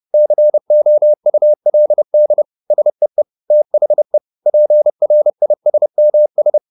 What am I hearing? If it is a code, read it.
Morse code, heard as a keyed tone, about 30 wpm, 600 Hz. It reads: COULD SEE THE PRISMS
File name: the old man and the sea / the old man and the sea 1 30wpm